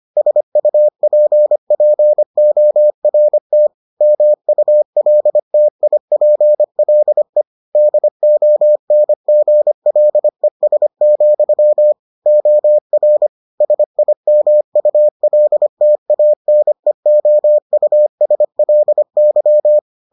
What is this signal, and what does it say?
Morse code, heard as a keyed tone, about 25 wpm, 600 Hz. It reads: SUPPORT MULTIPLE DONGLES, OR SIMULTANEOUSLY